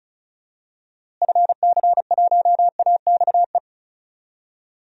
Morse code, heard as a keyed tone, about 36 wpm, 700 Hz. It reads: FC1AXE
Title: DXCC-100 Call Signs - 6 Characters - Repeated 3x - 35wpm